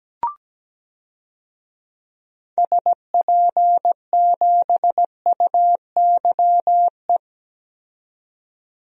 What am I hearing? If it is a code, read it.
Morse code, heard as a keyed tone, about 17 wpm, 700 Hz. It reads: SP7UYE
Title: DXCC-100 Call Signs - 6 Characters - 17wpm